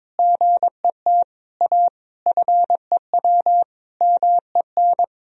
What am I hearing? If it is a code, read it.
Morse code, heard as a keyed tone, about 22 wpm, 700 Hz. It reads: GET A FEW MEN
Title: Sentences from Top 1000 Words - Review Part 1 - 22wpm